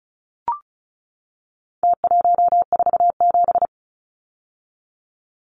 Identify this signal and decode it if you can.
Morse code, heard as a keyed tone, about 34 wpm, 700 Hz. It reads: T147
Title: Numbers - Contest Style - 4-Digit 35wpm